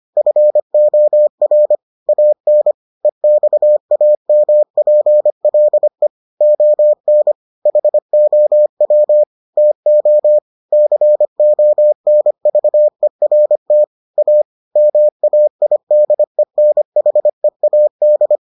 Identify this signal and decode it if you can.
Morse code, heard as a keyed tone, about 25 wpm, 600 Hz. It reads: FOR AN EXAMPLE ON HOW TO CONVERT A MAIDENHEAD